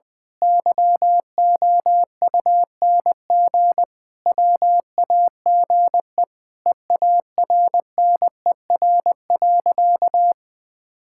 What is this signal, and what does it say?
Morse code, heard as a keyed tone, about 21 wpm, 700 Hz. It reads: YOUNG WAGE EARNER.
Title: Pangram Challenge v2 - 20wpm